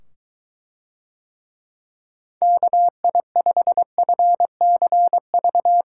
Morse code, heard as a keyed tone, about 23 wpm, 700 Hz. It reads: KI5FCV